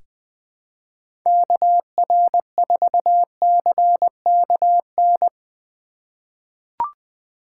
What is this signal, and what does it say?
Morse code, heard as a keyed tone, about 20 wpm, 700 Hz. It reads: KR4CKN
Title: USA Call Signs - 3x Character Spacing - 20wpm